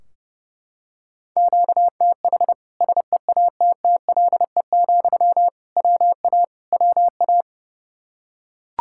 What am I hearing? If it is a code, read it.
Morse code, heard as a keyed tone, about 30 wpm, 700 Hz. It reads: QTH SEATTLE, WA WA
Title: Practice QSO - Exchange 1 - Beginning - 30wpm